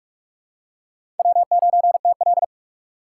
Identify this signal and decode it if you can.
Morse code, heard as a keyed tone, about 42 wpm, 700 Hz. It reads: W9TL
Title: USA Call Signs - 4 Characters - 4x Character Spacing - 45wpm